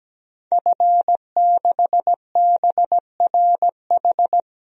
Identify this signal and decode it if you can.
Morse code, heard as a keyed tone, about 17 wpm, 700 Hz. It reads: F6BRH
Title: DXCC-100 Call Signs - 5 Characters - 17wpm